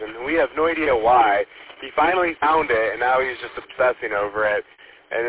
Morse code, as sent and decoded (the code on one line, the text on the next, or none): none